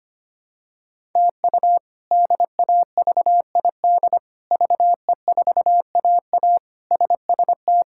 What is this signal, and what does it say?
Morse code, heard as a keyed tone, about 25 wpm, 700 Hz. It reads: TU DAVID VE4AA SST
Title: K1USN SST QSO Practice - 25wpm